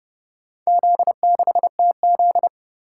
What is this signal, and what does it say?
Morse code, heard as a keyed tone, about 26 wpm, 700 Hz. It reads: Z6TZ